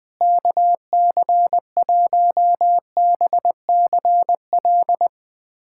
Morse code, heard as a keyed tone, about 20 wpm, 700 Hz. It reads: KC1BCL
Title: USA Callsign DE Callsign 20wpm